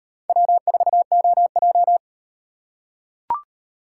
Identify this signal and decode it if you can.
Morse code, heard as a keyed tone, about 37 wpm, 700 Hz. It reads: W4OJ